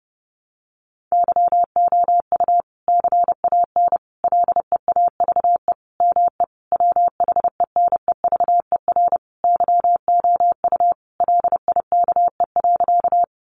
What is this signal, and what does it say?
Morse code, heard as a keyed tone, about 30 wpm, 700 Hz. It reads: YOU CAN LEAVE ME WHENEVER YOU LIKE.
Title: Sentences from Top 1200 Words 30wpm